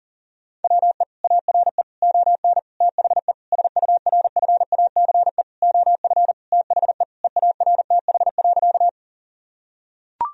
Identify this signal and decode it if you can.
Morse code, heard as a keyed tone, about 40 wpm, 700 Hz. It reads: WE ARE ON THE SURFACE OF THE EARTH.
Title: Sentences from Top 1000 Words - Review Part 3 - 40wpm